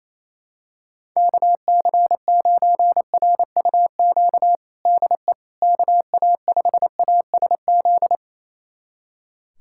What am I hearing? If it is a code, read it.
Morse code, heard as a keyed tone, about 28 wpm, 700 Hz. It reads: KC9RUQ DE KA5ASZ
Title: USA Callsign DE Callsign - 6 Characters - 28wpm